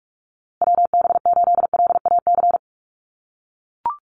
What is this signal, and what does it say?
Morse code, heard as a keyed tone, about 44 wpm, 700 Hz. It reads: WB8LAC
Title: USA Call Signs Repeated 2x 45wpm